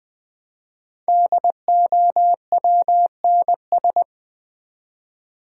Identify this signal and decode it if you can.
Morse code, heard as a keyed tone, about 20 wpm, 700 Hz. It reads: DOWNS
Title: Build A Word - 5 Letters At A Time - 10-Letter Words - 20wpm